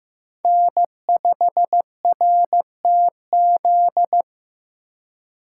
Morse code, heard as a keyed tone, about 15 wpm, 700 Hz. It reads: N5RTZ
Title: USA Call Signs - 5 Characters - 2x Character Spacing - 15wpm